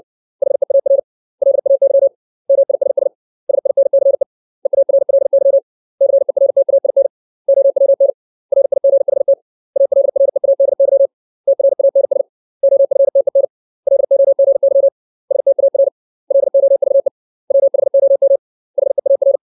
Morse code, heard as a keyed tone, about 60 wpm, 550 Hz. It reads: HEAR LAY MISS SINCE EARLY CERTAIN OWN RIGHT ALREADY TRANS OFTEN LOGY STAR LOVE WHOM HEAR
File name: 632 words / 632 Words 60WPM